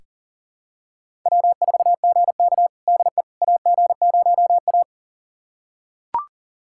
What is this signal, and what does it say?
Morse code, heard as a keyed tone, about 39 wpm, 700 Hz. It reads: W4GK DE AG0A
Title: USA Callsign DE Callsign - 4 Characters - 40wpm